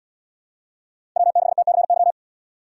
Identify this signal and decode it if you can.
Morse code, heard as a keyed tone, about 70 wpm, 700 Hz. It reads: W6TCY